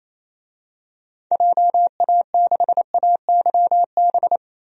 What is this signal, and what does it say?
Morse code, heard as a keyed tone, about 28 wpm, 700 Hz. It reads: JA6AYB
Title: DXCC-100 Call Signs - 6 Characters - Repeated 2x - 28wpm